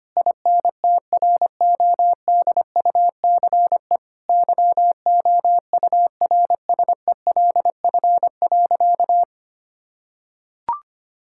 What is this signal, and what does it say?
Morse code, heard as a keyed tone, about 25 wpm, 700 Hz. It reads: INTRODUCE YOURSELF.
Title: Sentences from Top 1600 Words 25wpm